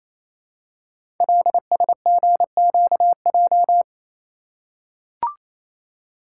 Morse code, heard as a keyed tone, about 29 wpm, 700 Hz. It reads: LSGQJ
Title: Code Groups - 5 Letter:Numbers - 28wpm